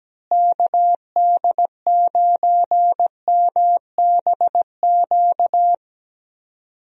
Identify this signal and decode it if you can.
Morse code, heard as a keyed tone, about 17 wpm, 700 Hz. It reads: KD9MBQ
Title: USA Call Signs - 6 Characters - Repeated 2x - 17wpm